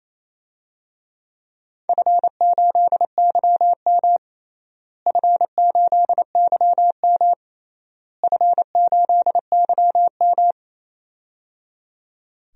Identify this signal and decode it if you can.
Morse code, heard as a keyed tone, about 27 wpm, 700 Hz. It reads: F8YM F8YM F8YM
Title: DXCC-100 Call Signs - 4 Characters - Repeated 3x - 28wpm